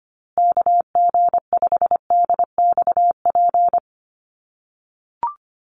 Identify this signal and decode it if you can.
Morse code, heard as a keyed tone, about 25 wpm, 700 Hz. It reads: KG5DXP
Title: USA Call Signs Repeated 3x 25wpm